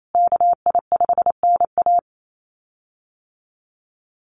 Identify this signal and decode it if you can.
Morse code, heard as a keyed tone, about 28 wpm, 700 Hz. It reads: KI5NA
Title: USA Call Signs Repeated 3x 28wpm